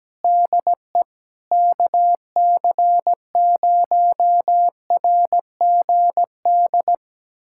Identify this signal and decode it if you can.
Morse code, heard as a keyed tone, about 17 wpm, 700 Hz. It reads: DE KC0RGD
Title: USA Callsign DE Callsign - 6 Characters - 17wpm